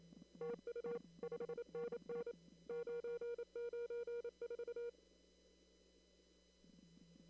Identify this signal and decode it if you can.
Morse code, heard as a keyed tone, about 25 wpm, 470 Hz. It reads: TU 5NN 994
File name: CW / kt5j wpx cw-op-k5pi